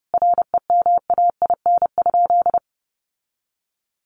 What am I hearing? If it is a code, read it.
Morse code, heard as a keyed tone, about 30 wpm, 700 Hz. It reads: REMAIN?
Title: Sentences from Top 1200 Words 30wpm